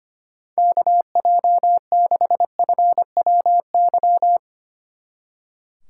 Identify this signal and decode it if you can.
Morse code, heard as a keyed tone, about 32 wpm, 700 Hz. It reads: KJ6FWY